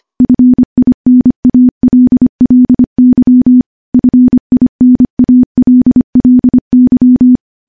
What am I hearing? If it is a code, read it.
Morse code, heard as a keyed tone, about 25 wpm, 250 Hz. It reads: FINALLY FINALLY
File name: morse / morse-training-25WPM